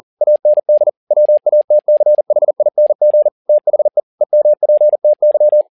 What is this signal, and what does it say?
Morse code, heard as a keyed tone, about 40 wpm, 600 Hz. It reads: AND WATCHING THE EMPTY
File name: learning center / tower trouble / tower touble 40wpm